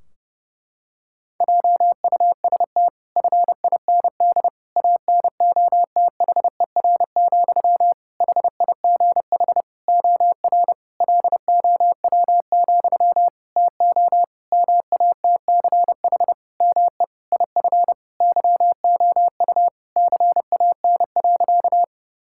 Morse code, heard as a keyed tone, about 32 wpm, 700 Hz. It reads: JUST FIND ANOTHER, HIGH OR LOW, TO MATCH ME IF YOU CAN.
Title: Sentences from Top 1600 Words 30wpm